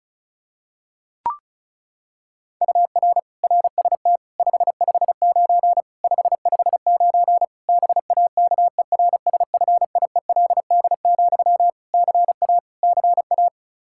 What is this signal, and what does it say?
Morse code, heard as a keyed tone, about 36 wpm, 700 Hz. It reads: UR RST 559 559 BAKERSFIELD, CA CA
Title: K1USN SST QSO Practice 35wpm